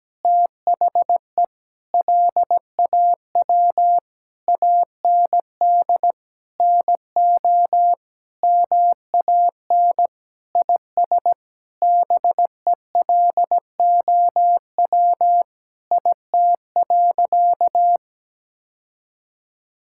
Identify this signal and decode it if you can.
Morse code, heard as a keyed tone, about 17 wpm, 700 Hz. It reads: THE LAW AND NO MAN IS BELOW IT.
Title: Sentences from Top 1000 Words - Review Part 5 - 17wpm